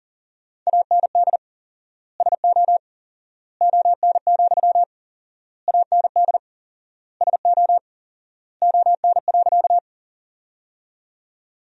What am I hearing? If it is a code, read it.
Morse code, heard as a keyed tone, about 40 wpm, 700 Hz. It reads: AND SO ON, AND SO ON.
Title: 4x Word-Spacing - Sentences from Top 100 Words - 40wpm